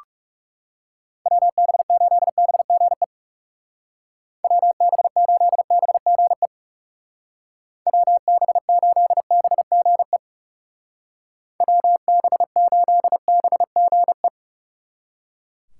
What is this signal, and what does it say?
Morse code, heard as a keyed tone, about 36 wpm, 700 Hz. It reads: WB8BGE WB8BGE WB8BGE WB8BGE
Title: Speed Racing - USA 2x3 Call Signs 30wpm